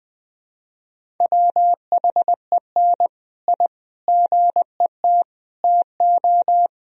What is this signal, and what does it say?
Morse code, heard as a keyed tone, about 20 wpm, 700 Hz. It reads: WHEN I GET TO
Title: Sentences from Top 1000 Words - Review Part 1 - 20wpm